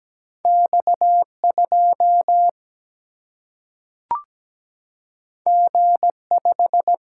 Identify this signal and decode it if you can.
Morse code, heard as a keyed tone, about 17 wpm, 700 Hz. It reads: X2 G5
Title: Single-Letter Single-Number 17wpm